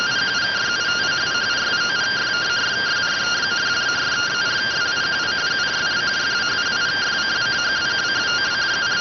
Two-tone teleprinter signal, about 170 Hz shift.